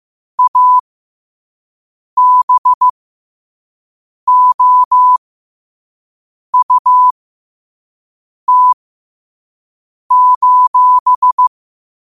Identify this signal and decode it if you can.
Morse code, heard as a keyed tone, about 15 wpm, spaced as for 5 wpm, 1000 Hz. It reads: ABOUT:
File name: qotdfiles / 5wpm